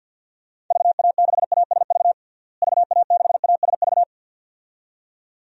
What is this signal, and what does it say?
Morse code, heard as a keyed tone, about 47 wpm, 700 Hz. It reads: VA6ASV VA6ASV
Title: DXCC-100 Call Signs - 6 Characters - Repeated 2x - 50wpm